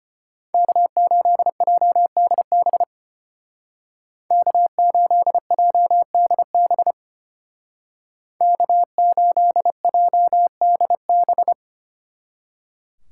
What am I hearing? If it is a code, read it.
Morse code, heard as a keyed tone, about 33 wpm, 700 Hz. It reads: K8JDB K8JDB K8JDB